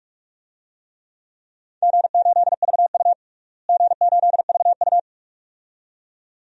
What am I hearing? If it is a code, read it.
Morse code, heard as a keyed tone, about 46 wpm, 700 Hz. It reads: G8VU G8VU
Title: DXCC-100 Call Signs - 4 Characters - Repeated 2x - 45wpm